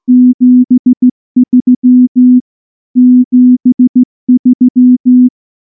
Morse code, heard as a keyed tone, about 14 wpm, 250 Hz. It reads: 73 73
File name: morse / morse-training-15WPM